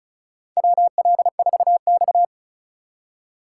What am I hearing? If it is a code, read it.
Morse code, heard as a keyed tone, about 36 wpm, 700 Hz. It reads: WL4X
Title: USA Call Signs - 4 Characters - 4x Character Spacing - 35wpm